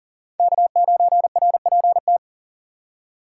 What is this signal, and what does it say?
Morse code, heard as a keyed tone, about 40 wpm, 690 Hz. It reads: K9RPT
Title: USA Call Signs - 5 Characters - 3x Character Spacing - 40wpm